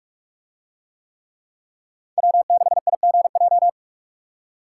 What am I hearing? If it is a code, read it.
Morse code, heard as a keyed tone, about 41 wpm, 700 Hz. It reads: W6IGJ